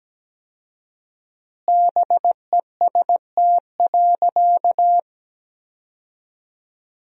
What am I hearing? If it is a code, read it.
Morse code, heard as a keyed tone, about 17 wpm, 700 Hz. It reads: BEST.
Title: 4x Word-Spacing - Sentences from Top 500 Words - 17wpm